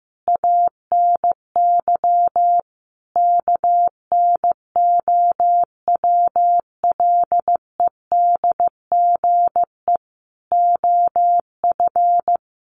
Morse code, heard as a keyed tone, about 15 wpm, 700 Hz. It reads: ANY KNOWLEDGE OF
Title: Sentences from Top 600 Words 15wpm